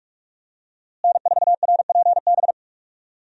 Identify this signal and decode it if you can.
Morse code, heard as a keyed tone, about 43 wpm, 700 Hz. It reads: N4RPB
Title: USA Call Signs - 5 Characters - Repeated 3x - 45wpm